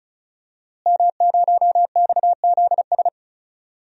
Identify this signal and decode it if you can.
Morse code, heard as a keyed tone, about 34 wpm, 700 Hz. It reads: M0XZS
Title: DXCC-100 Call Signs - 5 Characters - Repeated 2x - 35wpm